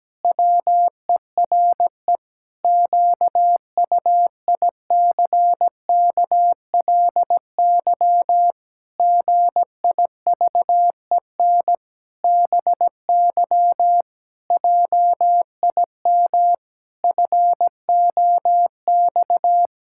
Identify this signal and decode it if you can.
Morse code, heard as a keyed tone, about 17 wpm, 700 Hz. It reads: WERE QUICKLY GIVEN BY JIM FOX
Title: Pangram Challenge v2 - 17wpm